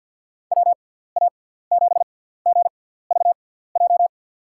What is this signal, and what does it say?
Morse code, heard as a keyed tone, about 49 wpm, spaced as for 17 wpm, 700 Hz. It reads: WA7GVJ
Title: USA Call Signs - 6 Characters - 6x Character Spacing - 50wpm